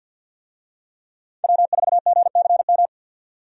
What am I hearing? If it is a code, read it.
Morse code, heard as a keyed tone, about 50 wpm, 700 Hz. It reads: W4GCM
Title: USA Call Signs - 5 Characters - 50wpm